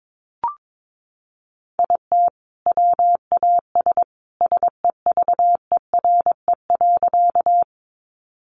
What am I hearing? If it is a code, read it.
Morse code, heard as a keyed tone, about 22 wpm, 700 Hz. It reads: IT WAS SEVERE.
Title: Sentences from Top 1400 Words 22wpm